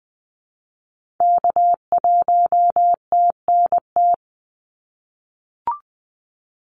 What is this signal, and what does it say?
Morse code, heard as a keyed tone, about 20 wpm, 700 Hz. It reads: K1TNT
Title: NA SOTA Activator Callsigns 20wpm